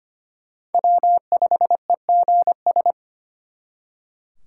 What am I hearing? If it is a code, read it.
Morse code, heard as a keyed tone, about 31 wpm, 700 Hz. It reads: W5EGS